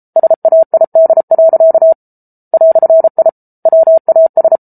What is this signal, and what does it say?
Morse code, heard as a keyed tone, about 34 wpm, 650 Hz. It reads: SAID. "I WAS